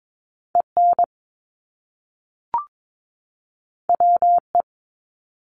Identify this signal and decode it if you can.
Morse code, heard as a keyed tone, about 22 wpm, 700 Hz. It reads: EN WE
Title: Top Two Letter Combinations in Words 22wpm